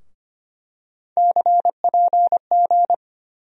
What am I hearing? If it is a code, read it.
Morse code, heard as a keyed tone, about 25 wpm, 700 Hz. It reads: CPG
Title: USA Call Sign Postfixes - 3 Letters 25wpm